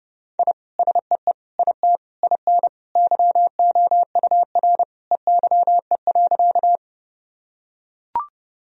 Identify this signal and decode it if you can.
Morse code, heard as a keyed tone, about 30 wpm, 700 Hz. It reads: I SEE IT IN YOUR EYE.
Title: Sentences from Top 1000 Words - Review Part 1 - 30wpm